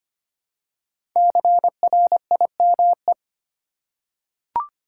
Morse code, heard as a keyed tone, about 25 wpm, 700 Hz. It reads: CRIME
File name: 5 Letter Words - First 3 Letters Given - 25wpm